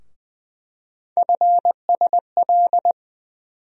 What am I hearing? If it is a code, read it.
Morse code, heard as a keyed tone, about 20 wpm, 700 Hz. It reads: FSL
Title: USA Call Sign Postfixes - 3 Letters 20wpm